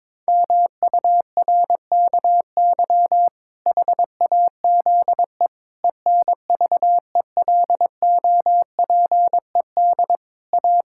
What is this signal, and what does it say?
Morse code, heard as a keyed tone, about 22 wpm, 700 Hz. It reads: MURKY HAZE ENVELOPED A
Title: Pangram Challenge v2 - 22wpm